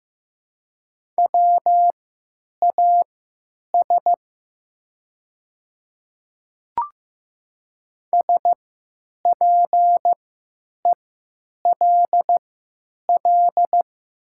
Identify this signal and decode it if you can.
Morse code, heard as a keyed tone, about 15 wpm, spaced as for 9 wpm, 700 Hz. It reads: WAS SPELL
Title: 3x Character Spacing - Top 200 Words - 15wpm